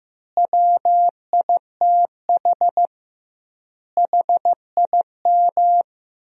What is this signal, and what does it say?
Morse code, heard as a keyed tone, about 15 wpm, 700 Hz. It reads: WITH HIM
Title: 2x Word-Spacing - Sentences from Top 200 Words - 15wpm